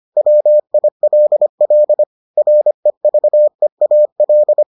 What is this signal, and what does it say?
Morse code, heard as a keyed tone, about 25 wpm, 600 Hz. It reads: WILL REVEAL